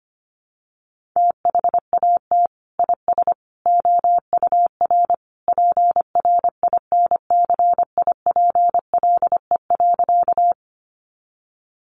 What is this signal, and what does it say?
Morse code, heard as a keyed tone, about 25 wpm, 700 Hz. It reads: THAT IS OUR PRINCIPLE.
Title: Sentences from Top 1400 Words 25wpm